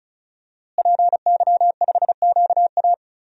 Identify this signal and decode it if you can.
Morse code, heard as a keyed tone, about 36 wpm, 700 Hz. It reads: PY5QA
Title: DXCC-100 Call Signs - 5 Characters - Repeated 3x - 35wpm